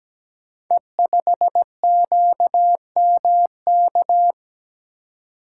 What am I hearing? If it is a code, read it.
Morse code, heard as a keyed tone, about 17 wpm, 700 Hz. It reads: E5QMK
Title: Code Groups - 5 Letter:Numbers - 17wpm